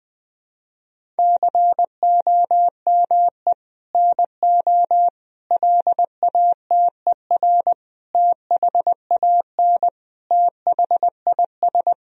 Morse code, heard as a keyed tone, about 20 wpm, 700 Hz. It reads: COME NO LATER THAN THIS
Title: Sentences from Top 1000 Words - Review Part 4 - 20wpm